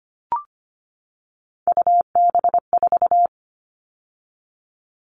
Morse code, heard as a keyed tone, about 25 wpm, 700 Hz. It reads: UB4
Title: All Cut Numbers - 3-Digit 25wpm